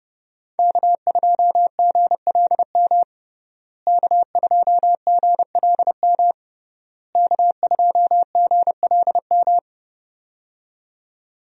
Morse code, heard as a keyed tone, about 30 wpm, 700 Hz. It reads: K2GLM K2GLM K2GLM
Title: USA Call Signs - 5 Characters - Repeated 3x - 30wpm